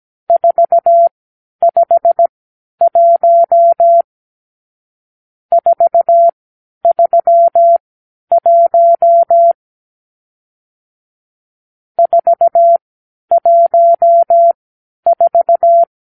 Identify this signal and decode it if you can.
Morse code, heard as a keyed tone, about 17 wpm, spaced as for 10 wpm, 680 Hz. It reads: 451 431 414
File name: k7qo / Lesson 52 (8)